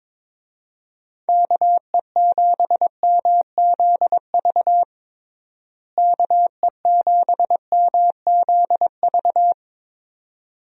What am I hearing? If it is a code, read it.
Morse code, heard as a keyed tone, about 21 wpm, 700 Hz. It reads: KE7MZV KE7MZV
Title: USA Call Signs - 6 Characters - Repeated 2x - 22wpm